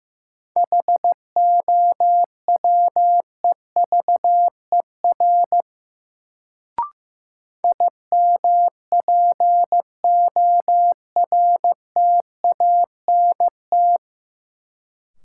Morse code, heard as a keyed tone, about 15 wpm, 700 Hz. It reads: HOWEVER IMPORTANT
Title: Sentences from Top 600 Words - New Word Drill - 15wpm